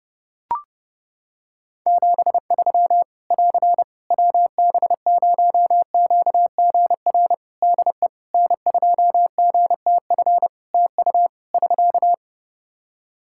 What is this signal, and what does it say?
Morse code, heard as a keyed tone, about 32 wpm, 700 Hz. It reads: 73 <AR> WB0QGR DE N2GTF TU <SK>
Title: Practice QSO - Exchange 4 - Ending - 30wpm